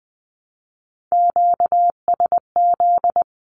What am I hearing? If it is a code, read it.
Morse code, heard as a keyed tone, about 20 wpm, 700 Hz. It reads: QSZ
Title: Q-Codes Repeated 3x 20wpm